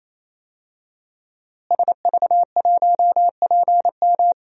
Morse code, heard as a keyed tone, about 27 wpm, 700 Hz. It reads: SV1PM
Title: DXCC-100 Call Signs - 5 Characters - Repeated 3x - 28wpm